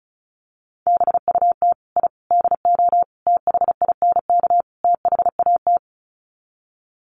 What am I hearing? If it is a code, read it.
Morse code, heard as a keyed tone, about 34 wpm, 700 Hz. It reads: BUT I DO THINK THAT
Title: Sets of 5 Words Encore2 35wpm